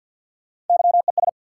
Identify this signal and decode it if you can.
Morse code, heard as a keyed tone, about 52 wpm, 700 Hz. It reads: YES
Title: Sentences from Top 500 Words - New Word Drill - 50wpm